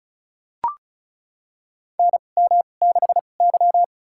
Morse code, heard as a keyed tone, about 36 wpm, spaced as for 25 wpm, 700 Hz. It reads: NM6Y
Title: USA Call Signs - 4 Characters - 2x Character Spacing - 35wpm